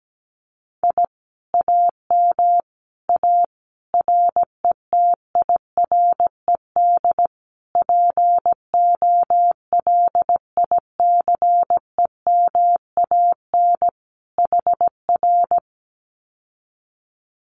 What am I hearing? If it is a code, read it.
Morse code, heard as a keyed tone, about 17 wpm, 700 Hz. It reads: I AM A RETIRED POLICEMAN HR
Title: Common QSO Phrases Encore 17wpm